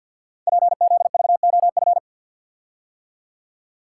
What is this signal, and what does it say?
Morse code, heard as a keyed tone, about 48 wpm, 690 Hz. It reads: PZVOF
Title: Code Groups - 5 Letters - 50wpm